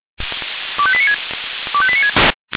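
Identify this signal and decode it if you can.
Five-tone selective call (ZVEI), heard as quick runs of five tones, tones of about 70 ms. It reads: 25896 25896
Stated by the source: ZVEI five-tone call 25431 received by SDR.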